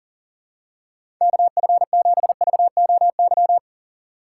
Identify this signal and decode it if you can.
Morse code, heard as a keyed tone, about 40 wpm, 700 Hz. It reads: KF7VOY
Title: USA Call Signs - 2x Character Spacing - 40wpm